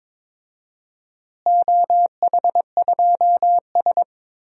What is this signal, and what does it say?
Morse code, heard as a keyed tone, about 22 wpm, 700 Hz. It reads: OH2S
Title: DXCC-100 Call Signs - 4 Characters - Repeated 2x - 22wpm